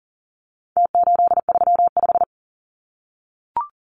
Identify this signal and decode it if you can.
Morse code, heard as a keyed tone, about 40 wpm, 700 Hz. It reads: T835
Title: Numbers - Contest Style - 4-Digit 40wpm